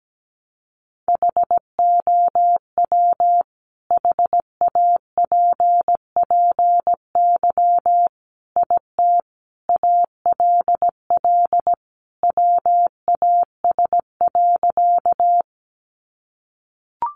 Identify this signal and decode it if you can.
Morse code, heard as a keyed tone, about 17 wpm, 700 Hz. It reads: HOW HAPPY IT ALL WAS.
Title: Sentences from Top 1000 Words 17wpm